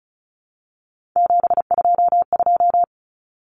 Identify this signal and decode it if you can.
Morse code, heard as a keyed tone, about 34 wpm, 700 Hz. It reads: 722
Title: Numbers - 3-Digit 35wpm